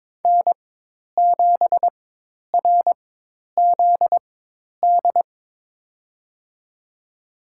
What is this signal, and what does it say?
Morse code, heard as a keyed tone, about 22 wpm, spaced as for 10 wpm, 700 Hz. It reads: N7RZD
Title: USA Call Signs - 5 Characters - 4x Character Spacing - 22wpm